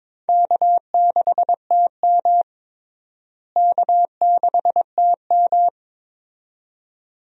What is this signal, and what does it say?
Morse code, heard as a keyed tone, about 21 wpm, 700 Hz. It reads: K6TM K6TM
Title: USA Call Signs - 4 Characters - Repeated 2x - 22wpm